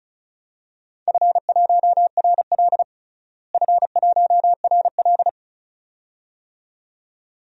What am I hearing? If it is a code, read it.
Morse code, heard as a keyed tone, about 34 wpm, 700 Hz. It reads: F1RL F1RL
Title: DXCC-100 Call Signs - 4 Characters - Repeated 2x - 35wpm